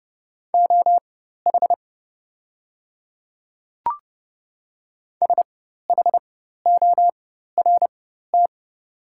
Morse code, heard as a keyed tone, about 29 wpm, spaced as for 14 wpm, 690 Hz. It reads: OH SHORT
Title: 4x Character Spacing - Top 500 Words - 30wpm